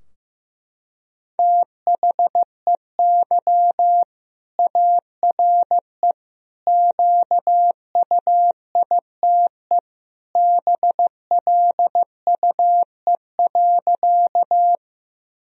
Morse code, heard as a keyed tone, about 15 wpm, 700 Hz. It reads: THEY ARE QUITE BLUE.
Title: Sentences from Top 1000 Words - Review Part 3 - 15wpm